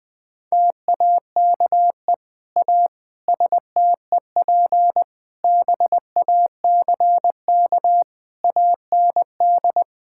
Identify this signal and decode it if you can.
Morse code, heard as a keyed tone, about 20 wpm, 700 Hz. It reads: TAKE A STEP BACK AND
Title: Sentences with 8-Word Phrases - 20wpm